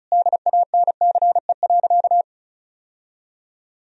Morse code, heard as a keyed tone, about 36 wpm, 700 Hz. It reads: DANCE.
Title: Sentences from Top 1000 Words - Review Part 4 - 35wpm